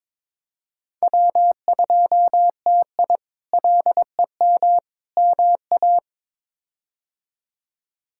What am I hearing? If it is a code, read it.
Morse code, heard as a keyed tone, about 22 wpm, 700 Hz. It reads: W2TI LEM MA
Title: K1USN SST QSO Practice - 22wpm